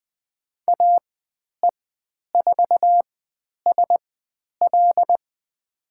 Morse code, heard as a keyed tone, about 20 wpm, spaced as for 10 wpm, 700 Hz. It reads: AE4SL
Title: 102 Lesson 17g - Letter u - Call Signs 20wpm